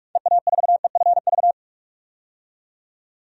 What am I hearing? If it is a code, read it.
Morse code, heard as a keyed tone, about 46 wpm, 700 Hz. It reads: EA4EFV
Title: DXCC-100 Call Signs - 6 Characters - 45wpm